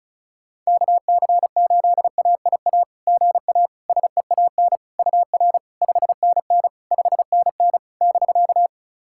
Morse code, heard as a keyed tone, about 36 wpm, 690 Hz. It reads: KC8AIA GA SEAN UR 5NN 5NN <BK>
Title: SOTA QSO Practice 35wpm